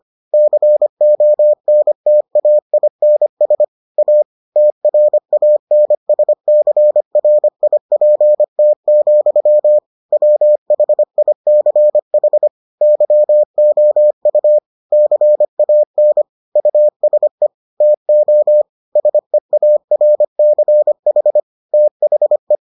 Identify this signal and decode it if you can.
Morse code, heard as a keyed tone, about 25 wpm, 600 Hz. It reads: CONTAINS A TRANSCRIPT, WHICH YOU CAN USE TO SEARCH THE